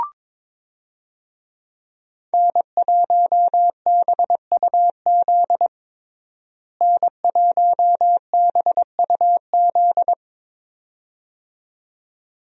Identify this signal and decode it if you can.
Morse code, heard as a keyed tone, about 22 wpm, 700 Hz. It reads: N1BUZ N1BUZ
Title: USA Call Signs - 5 Characters - Repeated 2x - 22wpm